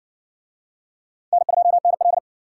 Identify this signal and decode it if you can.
Morse code, heard as a keyed tone, about 59 wpm, 700 Hz. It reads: N2NL